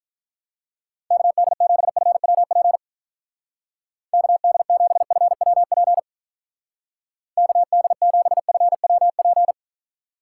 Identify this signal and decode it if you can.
Morse code, heard as a keyed tone, about 45 wpm, 700 Hz. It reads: KD7FWP KD7FWP KD7FWP